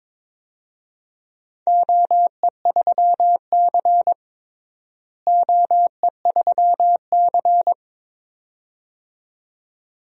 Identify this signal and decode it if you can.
Morse code, heard as a keyed tone, about 22 wpm, 700 Hz. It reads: OE3C OE3C
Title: DXCC-100 Call Signs - 4 Characters - Repeated 2x - 22wpm